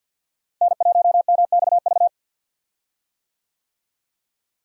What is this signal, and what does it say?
Morse code, heard as a keyed tone, about 46 wpm, 700 Hz. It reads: N1MXV
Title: USA Call Signs - 5 Characters - Repeated 3x - 50wpm